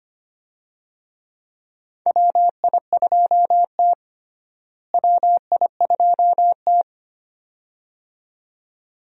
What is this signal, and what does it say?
Morse code, heard as a keyed tone, about 25 wpm, 700 Hz. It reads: WI2T WI2T
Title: USA Call Signs - 4 Characters - Repeated 2x - 25wpm